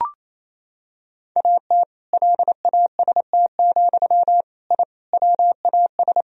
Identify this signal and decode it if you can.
Morse code, heard as a keyed tone, about 28 wpm, 700 Hz. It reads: AT LAST, I WAS
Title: Sentences from Top 1000 Words - Review Part 1 - 28wpm